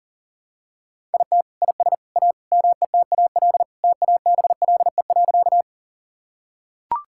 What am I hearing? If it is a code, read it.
Morse code, heard as a keyed tone, about 40 wpm, 700 Hz. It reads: IT IS A METAL TABLE.
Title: Sentences from Top 1000 Words - Review Part 6 - 40wpm